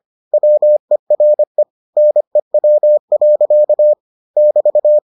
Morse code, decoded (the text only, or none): WERE NEW. <BT>